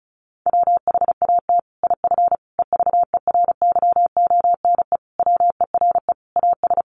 Morse code, heard as a keyed tone, about 34 wpm, 700 Hz. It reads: WHAT IF EVERYONE WERE AS
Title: Sentences from Top 1400 Words 35wpm